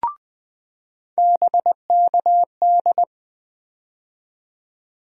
Morse code, heard as a keyed tone, about 20 wpm, 700 Hz. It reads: BKD
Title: USA Call Sign Postfixes - 3 Letters 20wpm